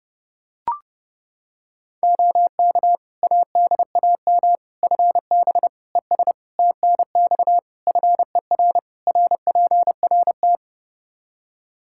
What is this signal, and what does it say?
Morse code, heard as a keyed tone, about 30 wpm, 700 Hz. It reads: OK ADAM FB ES TNX FER RPRT
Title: Practice QSO - Exchange 2 - Info - 30wpm